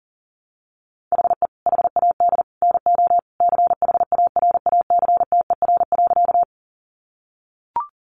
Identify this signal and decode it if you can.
Morse code, heard as a keyed tone, about 37 wpm, 700 Hz. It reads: HE HAD NO CHARACTER.
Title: Sentences from Top 1200 Words 40wpm